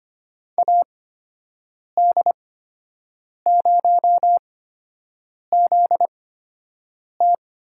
Morse code, heard as a keyed tone, about 25 wpm, spaced as for 7 wpm, 700 Hz. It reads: AD0ZT
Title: USA Call Signs - 5 Characters - 8x Character Spacing - 25wpm